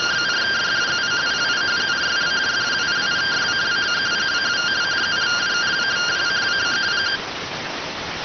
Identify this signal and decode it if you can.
Two-tone teleprinter signal, about 170 Hz shift.